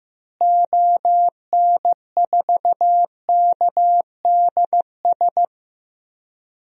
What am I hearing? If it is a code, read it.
Morse code, heard as a keyed tone, about 15 wpm, 700 Hz. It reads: ON4KDS
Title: DXCC-100 Call Signs - 6 Characters - Repeated 3x - 15wpm